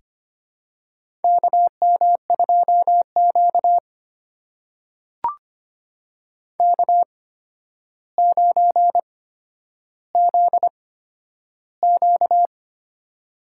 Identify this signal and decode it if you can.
Morse code, heard as a keyed tone, about 25 wpm, spaced as for 11 wpm, 700 Hz. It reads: KM2Q K9ZQ
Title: USA Call Signs - 4 Characters - 8x Character Spacing - 25wpm